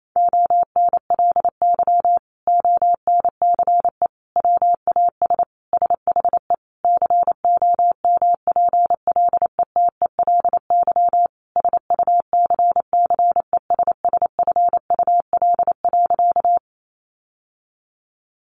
Morse code, heard as a keyed tone, about 28 wpm, 700 Hz. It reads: ONLY ONCE WAS SHE COMPLETELY SUCCESSFUL.
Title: Sentences from Top 1100 Words 28wpm